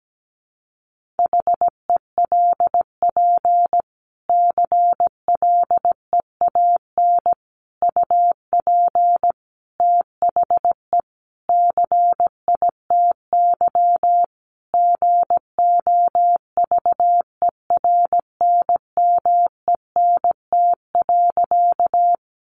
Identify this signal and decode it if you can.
Morse code, heard as a keyed tone, about 17 wpm, 700 Hz. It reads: HELP CLEAN UP THE CITY GOVERNMENT.
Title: Sentences from Top 1100 Words 17wpm